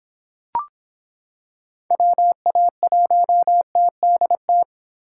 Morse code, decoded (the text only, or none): WA1TDT